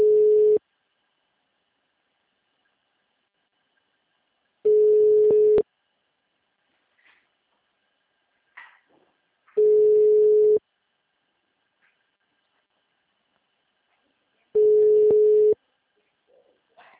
Ringback tone (Europe, 425 Hz).